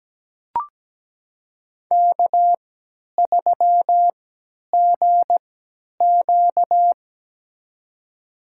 Morse code, heard as a keyed tone, about 17 wpm, spaced as for 10 wpm, 700 Hz. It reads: K3GQ
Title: USA Call Signs - 3x Character Spacing - 17wpm